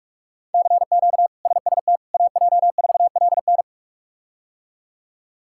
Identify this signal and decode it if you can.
Morse code, heard as a keyed tone, about 46 wpm, 700 Hz. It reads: CQ SST AJ4LN
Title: K1USN SST QSO Practice - 45wpm